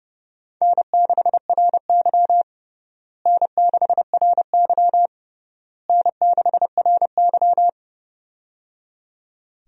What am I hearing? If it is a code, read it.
Morse code, heard as a keyed tone, about 30 wpm, 700 Hz. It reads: N6RY N6RY N6RY
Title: USA Call Signs - 4 Characters - Repeated 3x - 30wpm